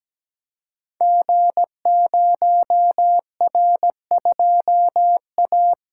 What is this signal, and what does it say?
Morse code, heard as a keyed tone, about 17 wpm, 700 Hz. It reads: G0R2A
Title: Code Groups - 5 Letter:Numbers - 17wpm